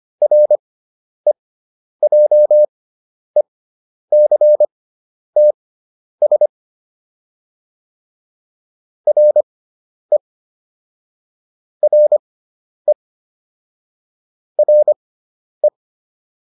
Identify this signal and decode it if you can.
Morse code, heard as a keyed tone, about 25 wpm, spaced as for 10 wpm, 600 Hz. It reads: REJECTS RE RE RE